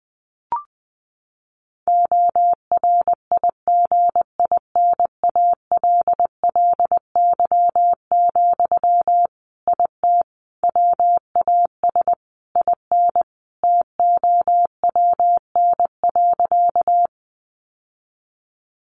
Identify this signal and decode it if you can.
Morse code, heard as a keyed tone, about 20 wpm, 700 Hz. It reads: ORIGINALLY, IT WAS IN TOWN.
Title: Sentences from Top 1500 Words 20wpm